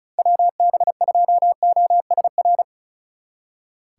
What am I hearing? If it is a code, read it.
Morse code, heard as a keyed tone, about 43 wpm, 700 Hz. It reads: WB2OSR
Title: Speed Racing - USA 2x3 Call Signs 35wpm